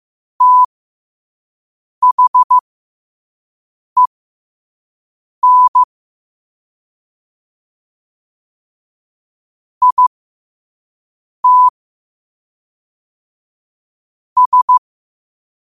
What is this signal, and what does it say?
Morse code, heard as a keyed tone, about 15 wpm, spaced as for 5 wpm, 1000 Hz. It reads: THEN IT S
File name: qotdfiles / 5wpm